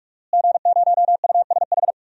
Morse code, heard as a keyed tone, about 41 wpm, 700 Hz. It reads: G0USH
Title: DXCC-100 Call Signs - 5 Characters - Repeated 2x - 45wpm